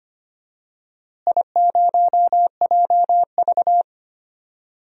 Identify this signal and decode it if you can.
Morse code, heard as a keyed tone, about 25 wpm, 700 Hz. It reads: I0JV
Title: DXCC-100 Call Signs - 4 Characters - Repeated 2x - 25wpm